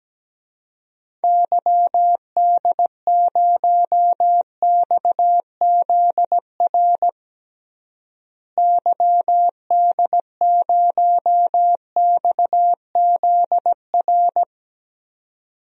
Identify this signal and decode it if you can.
Morse code, heard as a keyed tone, about 17 wpm, 700 Hz. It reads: YD0XZR YD0XZR
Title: DXCC-100 Call Signs - 6 Characters - Repeated 2x - 17wpm